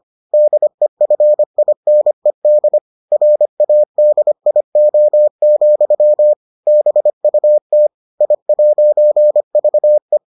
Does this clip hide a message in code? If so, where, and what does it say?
Yes: Morse code from 0.3 to 10.2 s, heard as a keyed tone, about 25 wpm, 600 Hz. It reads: DEFINED RADIO, BUT I'VE